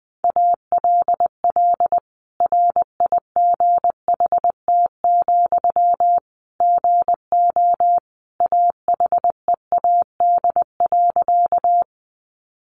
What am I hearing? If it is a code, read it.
Morse code, heard as a keyed tone, about 20 wpm, 700 Hz. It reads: ALL RIGHT, GO AHEAD.
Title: Sentences from Top 1100 Words 20wpm